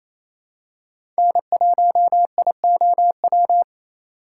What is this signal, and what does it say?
Morse code, heard as a keyed tone, about 28 wpm, 700 Hz. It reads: N1IOW
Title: USA Call Signs - 5 Characters - 4x Character Spacing - 28wpm